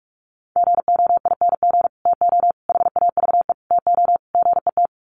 Morse code, heard as a keyed tone, about 44 wpm, 700 Hz. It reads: GOING TO HAVE TO GET
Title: Sets of 5 Words Encore2 45wpm